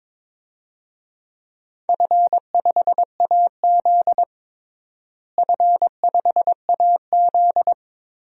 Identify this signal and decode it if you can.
Morse code, heard as a keyed tone, about 22 wpm, 700 Hz. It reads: F5AZ F5AZ
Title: DXCC-100 Call Signs - 4 Characters - Repeated 2x - 22wpm